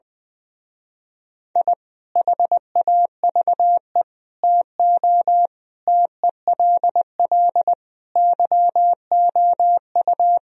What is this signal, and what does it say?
Morse code, heard as a keyed tone, about 20 wpm, 700 Hz. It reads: I HAVE TO TELL YOU
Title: Speed-Racing - Sets of 5 Words 20wpm